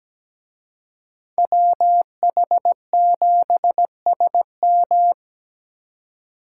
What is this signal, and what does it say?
Morse code, heard as a keyed tone, about 20 wpm, 700 Hz. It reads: WH7SM
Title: Speed Racing - USA 2x2 Call Signs 17wpm